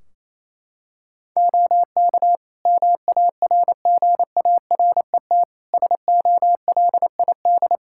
Morse code, heard as a keyed tone, about 28 wpm, 700 Hz. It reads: OK MARGARET SOLID
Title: Practice QSO - Exchange 3 - More Info - 28wpm